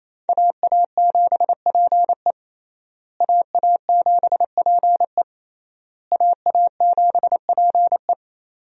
Morse code, heard as a keyed tone, about 28 wpm, 700 Hz. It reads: AA7PE AA7PE AA7PE
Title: USA Call Signs - 5 Characters - Repeated 3x - 28wpm